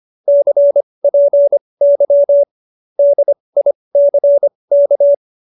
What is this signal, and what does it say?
Morse code, heard as a keyed tone, about 25 wpm, spaced as for 20 wpm, 570 Hz. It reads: CPY DICK
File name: QSO302 20